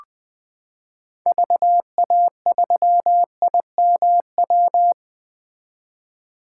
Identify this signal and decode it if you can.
Morse code, heard as a keyed tone, about 20 wpm, 700 Hz. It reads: VA3IMW
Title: 2023 POTA Hunter Call Signs - 20wpm